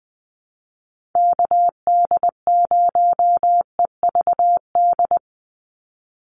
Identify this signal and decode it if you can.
Morse code, heard as a keyed tone, about 20 wpm, 700 Hz. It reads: KD0EVD
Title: USA Call Signs 20wpm